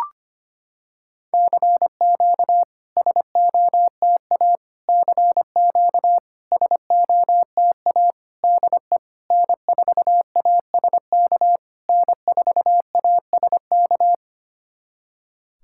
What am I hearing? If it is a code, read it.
Morse code, heard as a keyed tone, about 25 wpm, 690 Hz. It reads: CQ SOTA CQ SOTA DE N4ASK N4ASK
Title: SOTA QSO Practice 25wpm